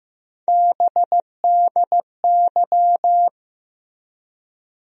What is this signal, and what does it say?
Morse code, heard as a keyed tone, about 15 wpm, 700 Hz. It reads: BDY
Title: USA Call Sign Postfixes - 3 Letters 15wpm